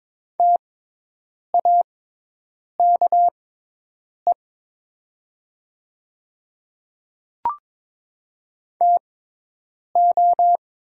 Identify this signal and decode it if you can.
Morse code, heard as a keyed tone, about 22 wpm, spaced as for 8 wpm, 700 Hz. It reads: TAKE TO
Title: 6x Character Spacing - Top 200 Words - 22wpm